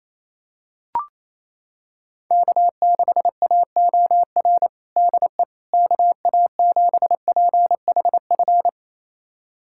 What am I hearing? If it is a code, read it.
Morse code, heard as a keyed tone, about 28 wpm, 700 Hz. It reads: K6AOR DE KA7PHF
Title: USA Callsign DE Callsign 28wpm